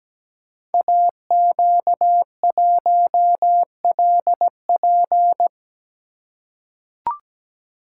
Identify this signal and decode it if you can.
Morse code, heard as a keyed tone, about 17 wpm, 700 Hz. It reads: AQ1LP
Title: Code Groups - 5 Letter:Numbers - 17wpm